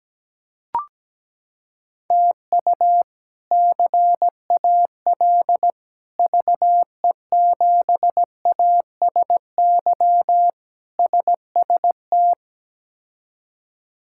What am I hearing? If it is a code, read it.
Morse code, heard as a keyed tone, about 17 wpm, 700 Hz. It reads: TU CAL VE7ASY SST
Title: K1USN SST QSO Practice - 17wpm